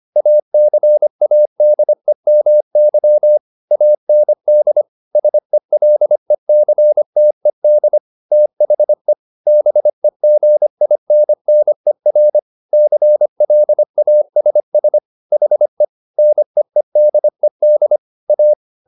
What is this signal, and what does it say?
Morse code, heard as a keyed tone, about 25 wpm, 600 Hz. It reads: ACADEMY AND SELECTED THE BEGINNER CLASS HE NEEDED A